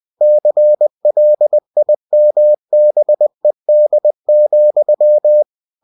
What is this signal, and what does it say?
Morse code, heard as a keyed tone, about 20 wpm, 600 Hz. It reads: CLIMBED,
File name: cw-books / winnie-the-pooh / winnie-20wpm-page-0003